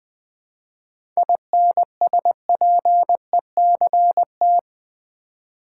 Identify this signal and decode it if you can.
Morse code, heard as a keyed tone, about 20 wpm, 690 Hz. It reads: INSPECT
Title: 7 Letter Words - First 3 Letters Given - 20wpm